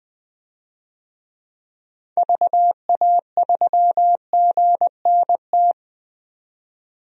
Morse code, heard as a keyed tone, about 21 wpm, 700 Hz. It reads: VA3GNT
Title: DXCC-100 Call Signs - 6 Characters - 20wpm